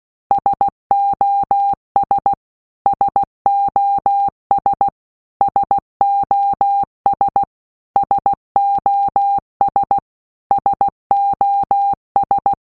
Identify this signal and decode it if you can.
Morse code, heard as a keyed tone, about 16 wpm, 800 Hz. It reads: SOS SOS SOS SOS SOS
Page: SOS sound in Morse Code (Slow)